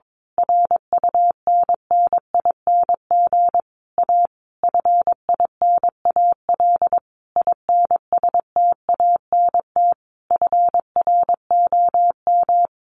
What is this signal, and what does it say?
Morse code, heard as a keyed tone, about 22 wpm, 700 Hz. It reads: RUNNING A FINAL INSTANT FROM
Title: A Sound Of Thunder 22wpm